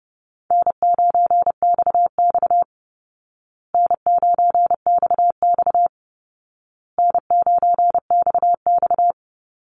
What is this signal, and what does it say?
Morse code, heard as a keyed tone, about 30 wpm, 700 Hz. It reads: N9XX N9XX N9XX
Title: Call Signs Repeated 2x - 30wpm